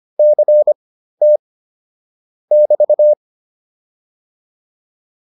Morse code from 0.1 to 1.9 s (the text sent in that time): CT